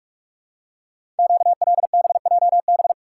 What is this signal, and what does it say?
Morse code, heard as a keyed tone, about 44 wpm, 690 Hz. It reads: QLBJB
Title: Code Groups - 5 Letters - 45wpm